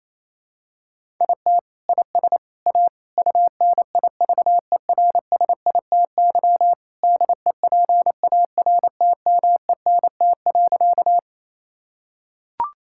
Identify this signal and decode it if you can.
Morse code, heard as a keyed tone, about 28 wpm, 700 Hz. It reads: IT IS A UNIVERSITY DEPARTMENT.
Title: Sentences from Top 1000 Words - Review Part 5 - 28wpm